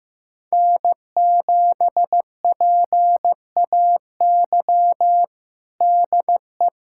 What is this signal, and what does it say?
Morse code, heard as a keyed tone, about 15 wpm, 700 Hz. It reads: N7PAY DE
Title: USA Callsign DE Callsign - 5 Characters - 15wpm